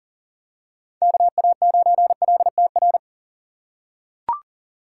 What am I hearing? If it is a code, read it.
Morse code, heard as a keyed tone, about 39 wpm, 700 Hz. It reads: KA9LTR
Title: USA Call Signs - 4x Character Spacing - 40wpm